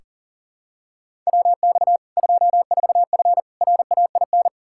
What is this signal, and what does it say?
Morse code, heard as a keyed tone, about 40 wpm, 700 Hz. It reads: WX 24F RAIN
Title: K1USN SST QSO Practice 40wpm